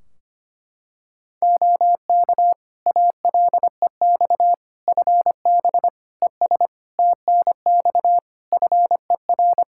Morse code, heard as a keyed tone, about 25 wpm, 700 Hz. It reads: OK ALEX FB ES TNX FER
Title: Practice QSO - Exchange 2 - Info - 25wpm